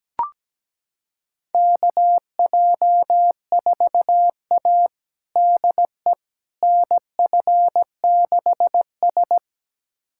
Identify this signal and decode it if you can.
Morse code, heard as a keyed tone, about 17 wpm, 700 Hz. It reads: KJ4A DE NF6S
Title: USA Callsign DE Callsign - 4 Characters - 17wpm